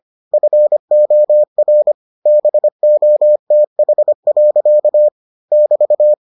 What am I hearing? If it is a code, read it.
Morse code, heard as a keyed tone, about 25 wpm, 600 Hz. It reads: FOR BOTH. <BT>